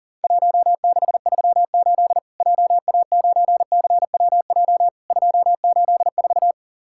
Morse code, heard as a keyed tone, about 40 wpm, 700 Hz. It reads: 1638 JA9CWJ 284
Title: Worked All Europe DX Contest - QTC Exchange - 40wpm